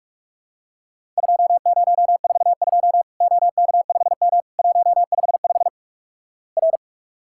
Morse code, heard as a keyed tone, about 46 wpm, 700 Hz. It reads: 2042 OK5M 155 R
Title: Worked All Europe DX Contest - QTC Exchange - 45wpm